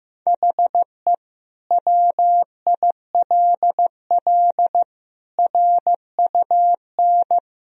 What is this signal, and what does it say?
Morse code, heard as a keyed tone, about 15 wpm, 700 Hz. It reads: HE WILL RUN
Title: Sentences from Top 1000 Words - Review Part 1 - 15wpm